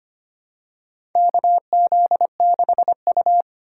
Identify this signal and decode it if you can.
Morse code, heard as a keyed tone, about 25 wpm, 700 Hz. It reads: KZ6U
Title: USA Call Signs - 4 Characters - 2x Character Spacing - 25wpm